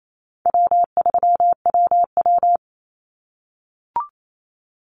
Morse code, heard as a keyed tone, about 28 wpm, 700 Hz. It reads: W3WW
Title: Call Signs - 28wpm